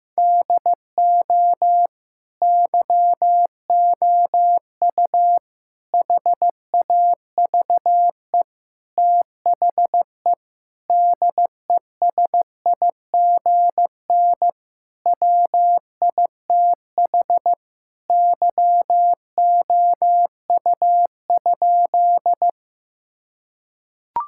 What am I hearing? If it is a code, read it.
Morse code, heard as a keyed tone, about 15 wpm, 700 Hz. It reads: DO YOU HAVE THE DESIGN WITH YOU?
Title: Sentences from Top 1000 Words - Review Part 5 - 15wpm